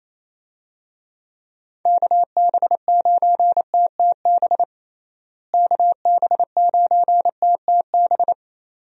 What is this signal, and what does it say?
Morse code, heard as a keyed tone, about 28 wpm, 700 Hz. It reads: KB9TTB KB9TTB
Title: USA Call Signs - 6 Characters - Repeated 2x - 28wpm